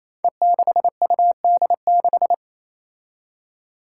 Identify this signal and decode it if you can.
Morse code, heard as a keyed tone, about 28 wpm, 700 Hz. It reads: E6UD6
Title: Five Challenging Letter-Number Combinations 28wpm